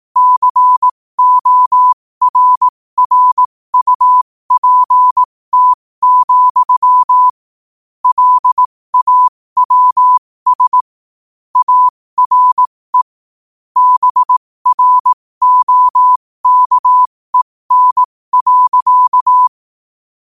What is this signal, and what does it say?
Morse code, heard as a keyed tone, about 18 wpm, spaced as for 15 wpm, 1000 Hz. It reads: CORRUPT, LAWS ARE BROKEN.
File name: qotdfiles / 15wpm